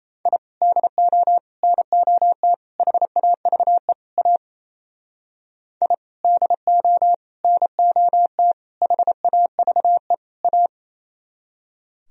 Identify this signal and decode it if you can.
Morse code, heard as a keyed tone, about 37 wpm, 700 Hz. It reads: I DO NOT HAVE A I DO NOT HAVE A